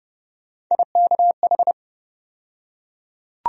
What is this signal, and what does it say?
Morse code, heard as a keyed tone, about 32 wpm, 700 Hz. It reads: IKH
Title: USA Call Sign Postfixes - 3 Letters 30wpm